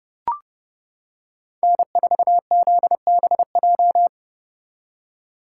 Morse code, heard as a keyed tone, about 29 wpm, 700 Hz. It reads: N4ZBJ